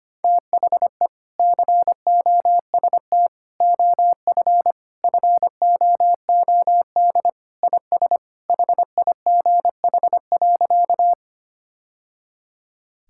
Morse code, heard as a keyed tone, about 25 wpm, 700 Hz. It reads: THE COST OF FOOD IS HIGH.
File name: practice / 25wpm / Sentences with 3-Word Phrases - 25wpm